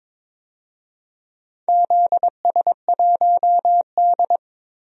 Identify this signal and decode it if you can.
Morse code, heard as a keyed tone, about 22 wpm, 700 Hz. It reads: ZS1D
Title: DXCC-100 Call Signs - 4 Characters - 22wpm